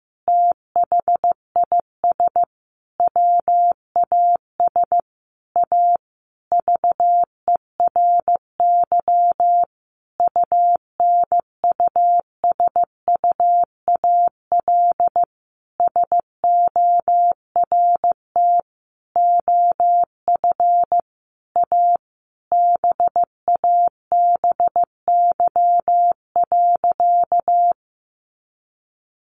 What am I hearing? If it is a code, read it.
Morse code, heard as a keyed tone, about 15 wpm, 700 Hz. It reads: THIS WAS A VERY UNUSUAL SORT OF A BABY.
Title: Sentences from Top 1400 Words 15wpm